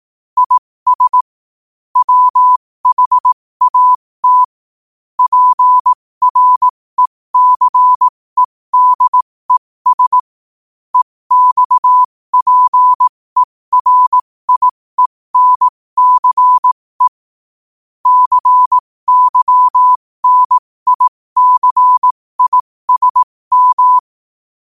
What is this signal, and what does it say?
Morse code, heard as a keyed tone, about 18 wpm, spaced as for 15 wpm, 1000 Hz. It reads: IS WHAT PRECEDES EXPERIENCE CYNICISM
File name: qotdfiles / 15wpm